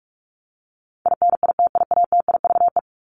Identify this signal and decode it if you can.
Morse code, heard as a keyed tone, about 44 wpm, 700 Hz. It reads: INITIATIVE
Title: Top 1751-2000 Words 45wpm